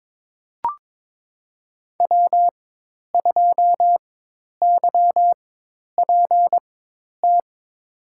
Morse code, heard as a keyed tone, about 22 wpm, spaced as for 10 wpm, 700 Hz. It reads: W2YPT
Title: USA Call Signs - 5 Characters - 4x Character Spacing - 22wpm